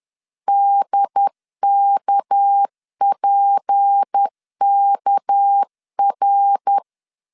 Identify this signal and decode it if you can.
Morse code, heard as a keyed tone, about 10 wpm, 800 Hz. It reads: DKPKR
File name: LEZIONI / 6 PKX0 / 6 PK 10wpm 2436